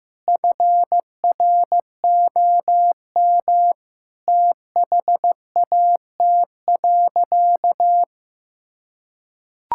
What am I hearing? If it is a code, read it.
Morse code, heard as a keyed tone, about 15 wpm, 700 Hz. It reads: FROM THAT.
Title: Sentences from Top 1000 Words - Review Part 5 - 15wpm